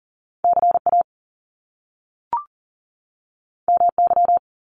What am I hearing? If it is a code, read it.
Morse code, heard as a keyed tone, about 40 wpm, 700 Hz. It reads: CA MY
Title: Top Two Letter Combinations in Words 40wpm